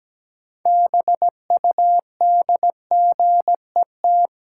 Morse code, heard as a keyed tone, about 17 wpm, 700 Hz. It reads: BUDGET
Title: Sentences from Top 1200 Words - New Word Drill Alt - 17wpm